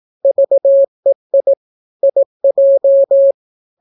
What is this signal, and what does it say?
Morse code, heard as a keyed tone, about 18 wpm, 550 Hz. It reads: VEI IJ